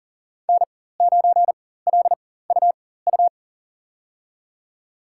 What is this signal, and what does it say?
Morse code, heard as a keyed tone, about 39 wpm, spaced as for 18 wpm, 700 Hz. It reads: N9LUU
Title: USA Call Signs - 5 Characters - 4x Character Spacing - 40wpm